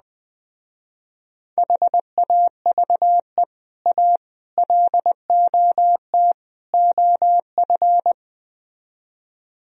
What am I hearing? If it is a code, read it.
Morse code, heard as a keyed tone, about 20 wpm, 700 Hz. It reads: HAVE A LOT OF
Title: Speed-Racing - Sets of 4 Words 17wpm